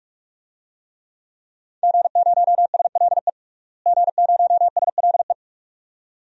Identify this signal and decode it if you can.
Morse code, heard as a keyed tone, about 42 wpm, 700 Hz. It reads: G0SLE G0SLE
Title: DXCC-100 Call Signs - 5 Characters - Repeated 2x - 45wpm